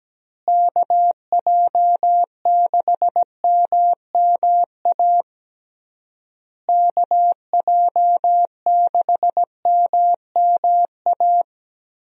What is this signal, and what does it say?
Morse code, heard as a keyed tone, about 17 wpm, 700 Hz. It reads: KJ6MMA KJ6MMA
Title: USA Call Signs - 6 Characters - Repeated 3x - 17wpm